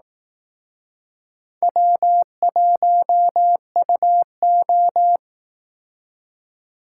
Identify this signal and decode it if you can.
Morse code, heard as a keyed tone, about 18 wpm, 700 Hz. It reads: W1UO
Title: Speed Racing - USA 1x2 Call Signs 15wpm